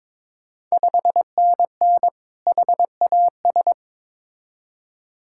Morse code, heard as a keyed tone, about 22 wpm, 690 Hz. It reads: 5NN HAS
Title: Ontario QSO Party Exchange 22wpm